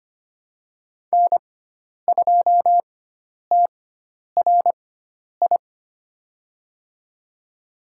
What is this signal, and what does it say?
Morse code, heard as a keyed tone, about 25 wpm, spaced as for 10 wpm, 700 Hz. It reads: N2TRI
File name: Lesson 16g - Letter c - Call Signs 25wpm